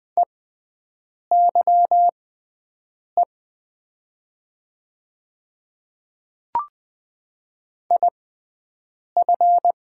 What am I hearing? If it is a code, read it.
Morse code, heard as a keyed tone, about 20 wpm, spaced as for 7 wpm, 700 Hz. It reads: EYE IF
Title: 6x Character Spacing - Top 500 Words - 20wpm